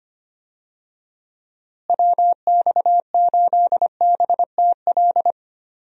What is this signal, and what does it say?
Morse code, heard as a keyed tone, about 25 wpm, 700 Hz. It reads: WX8BTL
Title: USA Call Signs - 6 Characters - Repeated 3x - 25wpm